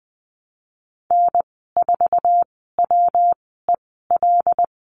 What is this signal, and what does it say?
Morse code, heard as a keyed tone, about 20 wpm, spaced as for 14 wpm, 700 Hz. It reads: N4WEL
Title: 193 Lesson 31g - Letter v - Call Signs 20wpm